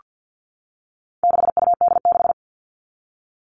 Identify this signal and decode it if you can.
Morse code, heard as a keyed tone, about 48 wpm, 700 Hz. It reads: 6UD6
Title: All Cut Numbers - 4-Digit 50wpm